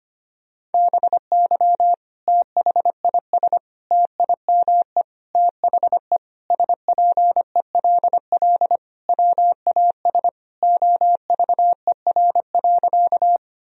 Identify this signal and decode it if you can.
Morse code, heard as a keyed tone, about 25 wpm, 700 Hz. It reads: BY THIS TIME THE SPELL WAS OVER.
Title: Sentences from Top 1000 Words - Review Part 1 - 25wpm